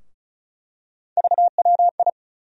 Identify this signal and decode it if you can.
Morse code, heard as a keyed tone, about 34 wpm, 700 Hz. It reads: VWI
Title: USA Call Sign Postfixes - 3 Letters 35wpm